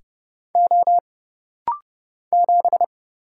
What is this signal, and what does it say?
Morse code, heard as a keyed tone, about 30 wpm, 700 Hz. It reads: O 7
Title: Single Letter-Number Alt - Rapid-Fire 30wpm